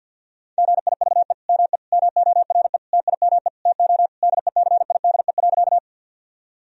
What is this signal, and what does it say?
Morse code, heard as a keyed tone, about 49 wpm, 700 Hz. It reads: GIVE ME MORE TIME TO DECIDE.
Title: Sentences from Top 1000 Words - Review Part 3 - 50wpm